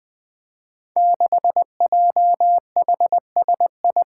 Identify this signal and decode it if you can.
Morse code, heard as a keyed tone, about 20 wpm, 700 Hz. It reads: 6JHSI
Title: Five Challenging Letter-Number Combinations 20wpm